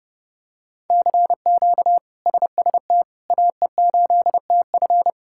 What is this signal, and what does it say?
Morse code, heard as a keyed tone, about 32 wpm, 700 Hz. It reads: CQ SST AE8TF
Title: K1USN SST QSO Practice - 30wpm